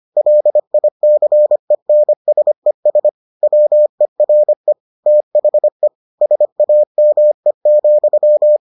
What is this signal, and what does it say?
Morse code, heard as a keyed tone, about 25 wpm, 600 Hz. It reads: LICENSES WERE THE SAME,